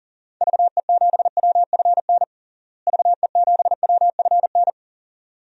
Morse code, heard as a keyed tone, about 40 wpm, 700 Hz. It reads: VE7WFN VE7WFN
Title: DXCC-100 Call Signs - 6 Characters - Repeated 2x - 40wpm